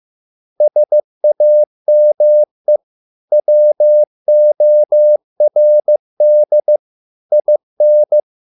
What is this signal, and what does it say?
Morse code, heard as a keyed tone, about 15 wpm, 600 Hz. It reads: SAME WORD IN